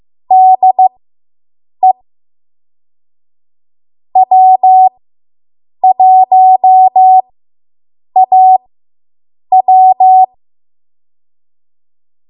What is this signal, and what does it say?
Morse code, heard as a keyed tone, about 15 wpm, spaced as for 7 wpm, 750 Hz. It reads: DE W1AW